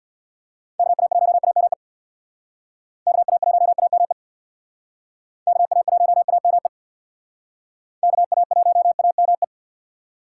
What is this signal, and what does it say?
Morse code, heard as a keyed tone, about 65 wpm, 700 Hz. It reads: KA1AME KA1AME KA1AME KA1AME